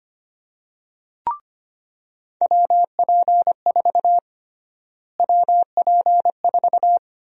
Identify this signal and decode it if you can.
Morse code, heard as a keyed tone, about 25 wpm, 700 Hz. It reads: WP4 WP4
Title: USA Call Sign Prefixes - Repeated 2x - 25wpm